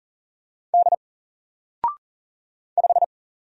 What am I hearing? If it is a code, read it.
Morse code, heard as a keyed tone, about 40 wpm, 700 Hz. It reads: D 5
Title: Single Character (Letter:Number:Punctuation:Prosign) 40wpm